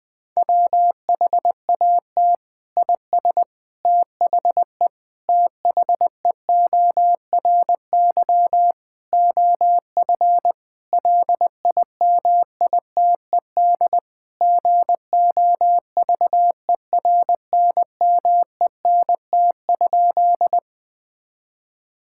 Morse code, heard as a keyed tone, about 21 wpm, 700 Hz. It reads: WHAT IS THE THEORY OF LIMITED GOVERNMENT?
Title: Sentences from Top 1000 Words - Review Part 6 - 20wpm